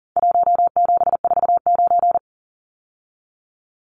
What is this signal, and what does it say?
Morse code, heard as a keyed tone, about 40 wpm, 700 Hz. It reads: 1749
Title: Numbers - Contest Style - 4-Digit 40wpm